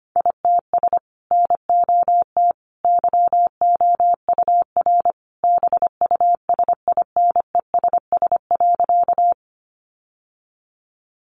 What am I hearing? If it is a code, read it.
Morse code, heard as a keyed tone, about 25 wpm, 700 Hz. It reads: ITS NOT YOUR BUSINESS.
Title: Sentences from Top 600 Words 25wpm